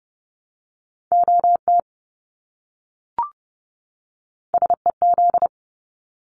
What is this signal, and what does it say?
Morse code, heard as a keyed tone, about 30 wpm, 700 Hz. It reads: OT SEZ
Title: Abbreviations 30wpm